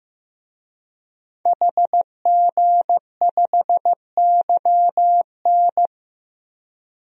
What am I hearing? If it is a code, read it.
Morse code, heard as a keyed tone, about 15 wpm, 700 Hz. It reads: HG5YN
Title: DXCC-100 Call Signs - 5 Characters - 15wpm